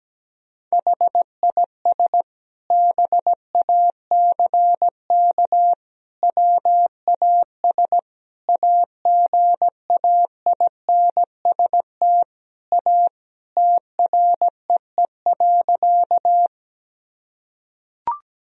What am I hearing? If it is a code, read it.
Morse code, heard as a keyed tone, about 17 wpm, 700 Hz. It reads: HIS BACK WAS AGAINST A TREE.
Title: Sentences from Top 1000 Words - Review Part 1 - 17wpm